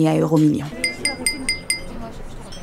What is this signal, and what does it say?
Speech. There is no code.